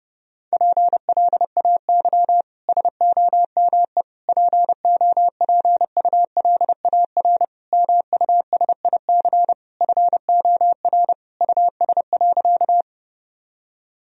Morse code, heard as a keyed tone, about 30 wpm, 700 Hz. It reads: PLAY SOME POPULAR MUSIC FOR US.
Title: Sentences from Top 1000 Words - Review Part 3 - 30wpm